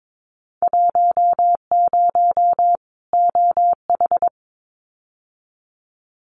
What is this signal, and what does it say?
Morse code, heard as a keyed tone, about 22 wpm, 700 Hz. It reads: 10 OH
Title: Winter Field Day Exchanges 22wpm